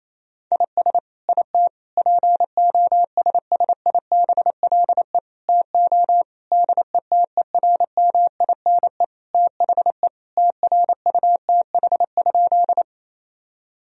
Morse code, heard as a keyed tone, about 28 wpm, 700 Hz. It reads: IS IT POSSIBLE TO DETERMINE THE TRUTH?
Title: Sentences from Top 1000 Words - Review Part 4 - 28wpm